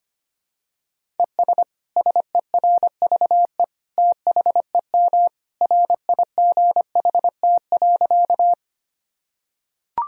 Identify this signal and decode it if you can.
Morse code, heard as a keyed tone, about 25 wpm, 700 Hz. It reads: ES SERVE THEM RIGHT.
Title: Sentences from Top 400 Words and Abbreviations 25wpm